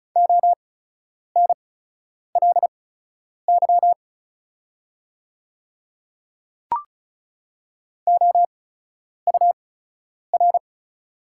Morse code, heard as a keyed tone, about 36 wpm, spaced as for 10 wpm, 690 Hz. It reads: ONLY OUR
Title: 8x Character Spacing - Top 300 Words - 35wpm